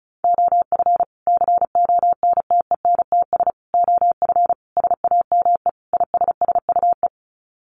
Morse code, heard as a keyed tone, about 34 wpm, 700 Hz. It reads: OF CONTENTS OF SAME ISSUE
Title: Sets of 5 Words Encore2 35wpm